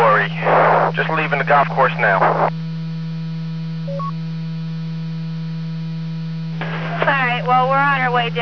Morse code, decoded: none